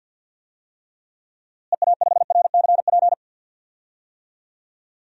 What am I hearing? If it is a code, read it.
Morse code, heard as a keyed tone, about 51 wpm, 700 Hz. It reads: EA5RCP
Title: DXCC-100 Call Signs - 6 Characters - Repeated 3x - 50wpm